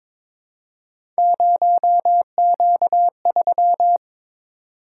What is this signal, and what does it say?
Morse code, heard as a keyed tone, about 22 wpm, 700 Hz. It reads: 0Q3
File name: practice / 22wpm / Three Letter-Number Combinations - 22wpm